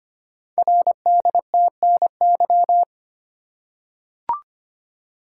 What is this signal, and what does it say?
Morse code, heard as a keyed tone, about 25 wpm, 700 Hz. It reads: RDTNY
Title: Code Groups - 5 Letters - 25wpm